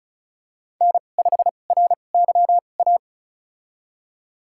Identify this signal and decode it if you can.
Morse code, heard as a keyed tone, about 36 wpm, spaced as for 25 wpm, 700 Hz. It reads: N5RYA
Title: USA Call Signs - 5 Characters - 2x Character Spacing - 35wpm